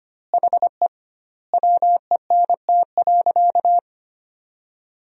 Morse code, heard as a keyed tone, about 25 wpm, 700 Hz. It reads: HE WENT.
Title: 2x Word-Spacing - Sentences from Top 200 Words - 25wpm